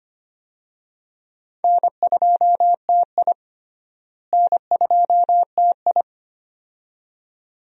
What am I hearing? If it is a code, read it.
Morse code, heard as a keyed tone, about 25 wpm, 700 Hz. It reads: N2TI N2TI
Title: USA Call Signs - 4 Characters - Repeated 2x - 25wpm